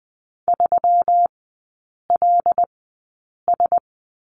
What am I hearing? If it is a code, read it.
Morse code, heard as a keyed tone, about 20 wpm, spaced as for 8 wpm, 700 Hz. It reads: 3LS
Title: 121 Lesson 20e - Number 3 - Three Characters 20wpm